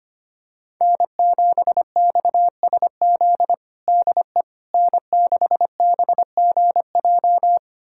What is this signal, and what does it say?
Morse code, heard as a keyed tone, about 25 wpm, 700 Hz. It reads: N7XSZ DE N6BGJ
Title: USA Callsign DE Callsign - 5 Characters - 25wpm